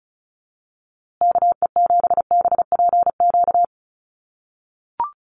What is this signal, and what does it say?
Morse code, heard as a keyed tone, about 34 wpm, 700 Hz. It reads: KE7BPQ
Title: USA Call Signs Repeated 2x 35wpm